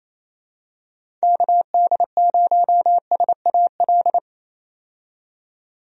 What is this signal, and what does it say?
Morse code, heard as a keyed tone, about 28 wpm, 700 Hz. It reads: KD0SAL
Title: USA Call Signs - 6 Characters - 28wpm